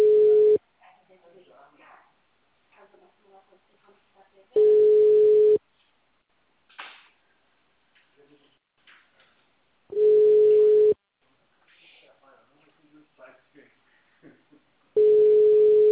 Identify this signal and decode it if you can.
Ringback tone (Europe, 425 Hz), with speech.
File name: out- -2007- - - .52758